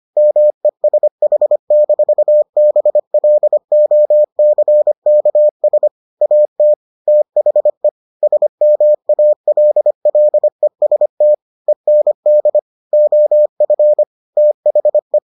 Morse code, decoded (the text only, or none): MESH-BLOCKS AT THE SMALLEST END OF THE